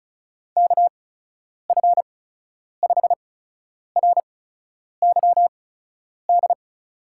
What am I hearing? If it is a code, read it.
Morse code, heard as a keyed tone, about 36 wpm, spaced as for 10 wpm, 700 Hz. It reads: KF5RYD
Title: USA Call Signs - 6 Characters - 8x Character Spacing - 35wpm